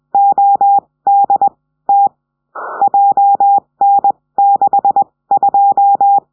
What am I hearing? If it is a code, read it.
Morse code, heard as a keyed tone, about 20 wpm, 800 Hz. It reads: O D T J N 6 2
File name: 70cm